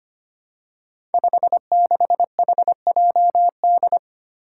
Morse code, heard as a keyed tone, about 26 wpm, 700 Hz. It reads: 56HJD
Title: Five Challenging Letter-Number Combinations 25wpm